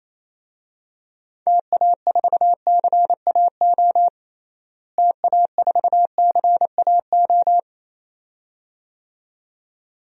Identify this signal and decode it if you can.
Morse code, heard as a keyed tone, about 28 wpm, 700 Hz. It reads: TA4CAO TA4CAO
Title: DXCC-100 Call Signs - 6 Characters - Repeated 2x - 28wpm